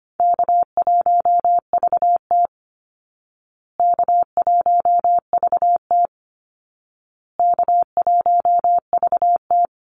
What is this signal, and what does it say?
Morse code, heard as a keyed tone, about 25 wpm, 700 Hz. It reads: K1VT K1VT K1VT
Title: Easy Call Signs Repeated 3x - 25wpm